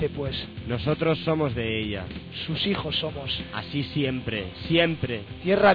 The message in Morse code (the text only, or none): none